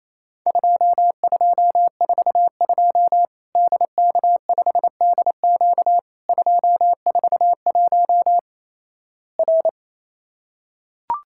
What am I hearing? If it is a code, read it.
Morse code, heard as a keyed tone, about 28 wpm, 700 Hz. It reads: 2242 DK5DQ 241 R
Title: Worked All Europe DX Contest - QTC Exchange - 28wpm